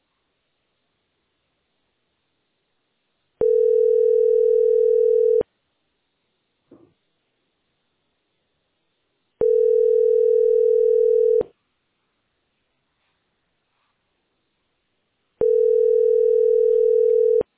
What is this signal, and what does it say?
Ringback tone (North America), 440 + 480 Hz.